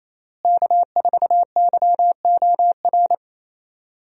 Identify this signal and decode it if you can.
Morse code, heard as a keyed tone, about 28 wpm, 700 Hz. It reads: K4YOR
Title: USA Call Signs - 5 Characters - Repeated 2x - 28wpm